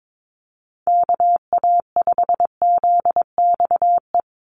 Morse code, heard as a keyed tone, about 22 wpm, 700 Hz. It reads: KA5ZXE
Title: USA Call Signs 22wpm